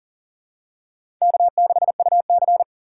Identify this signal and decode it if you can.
Morse code, heard as a keyed tone, about 37 wpm, 700 Hz. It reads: K6UC
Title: USA Call Signs - 4 Characters - 6x Character Spacing - 40wpm